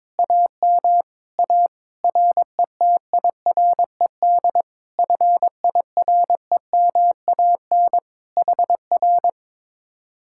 Morse code, heard as a keyed tone, about 22 wpm, 700 Hz. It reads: AM A RETIRED FIREMAN HR
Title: Speed-Racing Common QSO Phrases - 22wpm